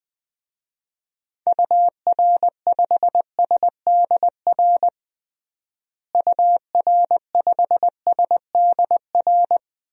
Morse code, heard as a keyed tone, about 21 wpm, 700 Hz. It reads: UR5SDR UR5SDR
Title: DXCC-100 Call Signs - 6 Characters - Repeated 3x - 20wpm